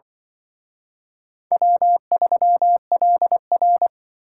Morse code, heard as a keyed tone, about 23 wpm, 700 Hz. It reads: W3LR
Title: Speed Racing - USA 1x2 Call Signs 20wpm